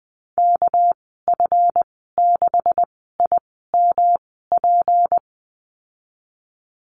Morse code, heard as a keyed tone, about 20 wpm, spaced as for 14 wpm, 700 Hz. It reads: KF6IMP
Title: 200 Lesson 32g - Letter k - Call Signs 20wpm